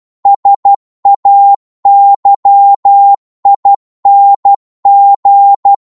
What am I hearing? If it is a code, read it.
Morse code, heard as a keyed tone, about 12 wpm, 800 Hz. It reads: SAYING